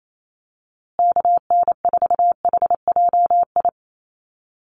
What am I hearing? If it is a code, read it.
Morse code, heard as a keyed tone, about 28 wpm, 700 Hz. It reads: KN4HJI
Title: USA Call Signs Repeated 2x 28wpm